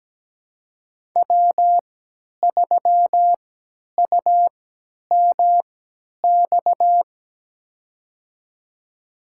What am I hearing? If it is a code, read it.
Morse code, heard as a keyed tone, about 17 wpm, spaced as for 10 wpm, 700 Hz. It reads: W3UMX
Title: USA Call Signs - 5 Characters - 3x Character Spacing - 17wpm